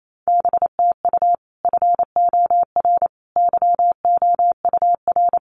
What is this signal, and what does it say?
Morse code, heard as a keyed tone, about 28 wpm, 700 Hz. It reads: BTU FOR YOUR
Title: Common QSO Phrases 28wpm